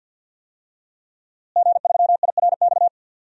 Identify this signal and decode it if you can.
Morse code, heard as a keyed tone, about 52 wpm, 700 Hz. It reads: G3IRX
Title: DXCC-100 Call Signs - 5 Characters - Repeated 3x - 50wpm